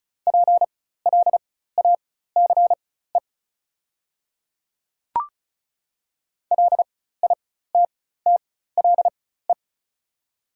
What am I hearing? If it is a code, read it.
Morse code, heard as a keyed tone, about 36 wpm, spaced as for 16 wpm, 690 Hz. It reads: PLACE LITTLE
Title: 4x Character Spacing - Top 200 Words - 35wpm